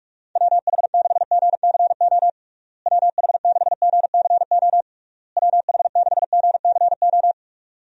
Morse code, heard as a keyed tone, about 46 wpm, 700 Hz. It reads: WH6GCO WH6GCO WH6GCO
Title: USA Call Signs - 6 Characters - Repeated 3x - 45wpm